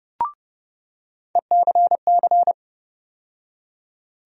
Morse code, heard as a keyed tone, about 30 wpm, 700 Hz. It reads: ECC
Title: USA Call Sign Postfixes - 3 Letters 30wpm